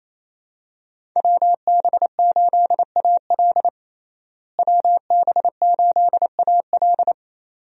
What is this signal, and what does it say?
Morse code, heard as a keyed tone, about 28 wpm, 700 Hz. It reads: WB8AL WB8AL
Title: USA Call Signs - 5 Characters - Repeated 2x - 28wpm